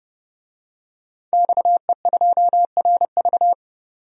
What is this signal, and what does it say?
Morse code, heard as a keyed tone, about 30 wpm, 700 Hz. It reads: XE2RV
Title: DXCC-100 Call Signs - 5 Characters - 30wpm